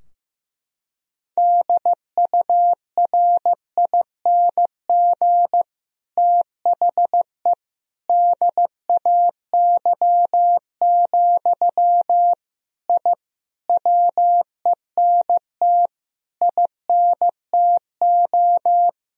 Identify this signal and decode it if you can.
Morse code, heard as a keyed tone, about 15 wpm, 700 Hz. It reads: DURING THE DAY, I WENT INTO
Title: Sentences from Top 1000 Words - Review Part 2 - 15wpm